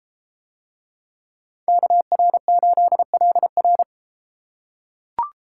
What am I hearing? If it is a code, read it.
Morse code, heard as a keyed tone, about 32 wpm, 700 Hz. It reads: KR8LR